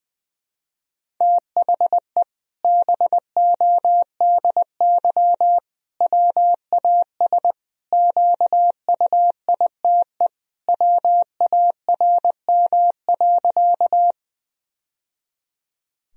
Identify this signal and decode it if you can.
Morse code, heard as a keyed tone, about 20 wpm, 700 Hz. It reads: THE BODY WAS QUITE WARM.
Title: Sentences from Top 1000 Words - Review Part 2 - 20wpm